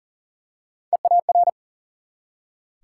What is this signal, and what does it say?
Morse code, heard as a keyed tone, about 40 wpm, 700 Hz. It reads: EAR
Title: Three Letter Words - 40wpm